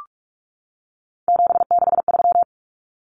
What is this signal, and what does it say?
Morse code, heard as a keyed tone, about 44 wpm, 700 Hz. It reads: 763
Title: Numbers - 3-Digit 45wpm